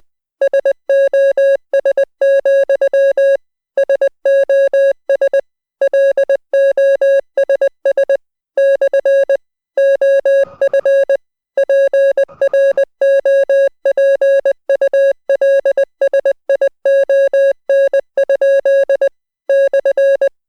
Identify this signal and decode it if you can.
Morse code, heard as a keyed tone, about 20 wpm, 550 Hz. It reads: SOS, SOS LOSS / OF PROPULSION? /